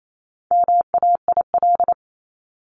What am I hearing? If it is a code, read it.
Morse code, heard as a keyed tone, about 28 wpm, 700 Hz. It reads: MAIL
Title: Top 1751-2000 Words 28wpm